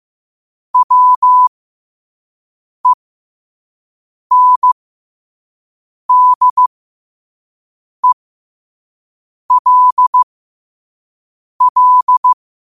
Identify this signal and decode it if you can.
Morse code, heard as a keyed tone, about 15 wpm, spaced as for 5 wpm, 1000 Hz. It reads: WENDELL
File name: qotdfiles / 5wpm